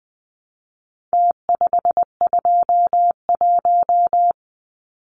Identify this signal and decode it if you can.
Morse code, heard as a keyed tone, about 20 wpm, 700 Hz. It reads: T521
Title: Numbers - Contest Style - 4-Digit 20wpm